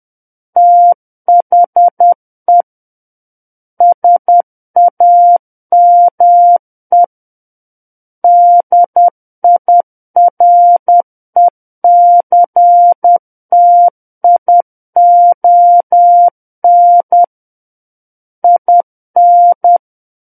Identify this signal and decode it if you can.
Morse code, heard as a keyed tone, about 10 wpm, 700 Hz. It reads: THE SAME DIRECTION IN
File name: Book One / 01-Chapter 1 - 10 WPM